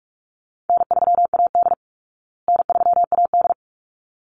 Morse code, heard as a keyed tone, about 44 wpm, 700 Hz. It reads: N3AD N3AD
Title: Call Signs Repeated 3x - 45wpm